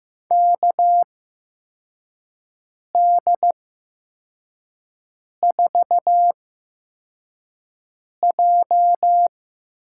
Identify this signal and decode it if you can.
Morse code, heard as a keyed tone, about 15 wpm, spaced as for 4 wpm, 700 Hz. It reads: KD4J
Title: USA Call Signs - 6 Characters - 8x Character Spacing - 15wpm